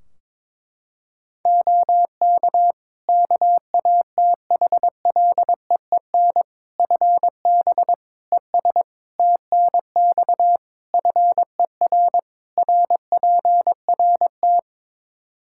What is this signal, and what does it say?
Morse code, heard as a keyed tone, about 22 wpm, 700 Hz. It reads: OK KATHLEEN FB ES TNX FER RPRT
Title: Practice QSO - Exchange 2 - Info - 22wpm